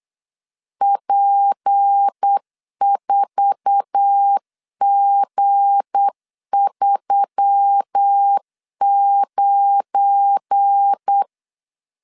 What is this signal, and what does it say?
Morse code, heard as a keyed tone, about 8 wpm, 800 Hz. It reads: P4G39
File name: CW / DET / 1 9 1500C S C 8wpm 5653